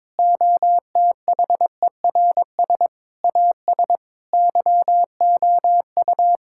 Morse code, heard as a keyed tone, about 22 wpm, 700 Hz. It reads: OTHERS AS YOU
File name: practice / 22wpm / Sets of 8 Words - 22wpm